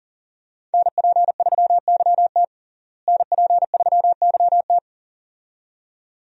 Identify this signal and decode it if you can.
Morse code, heard as a keyed tone, about 37 wpm, 700 Hz. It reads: NP3YT NP3YT
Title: USA Call Signs - 5 Characters - Repeated 2x - 40wpm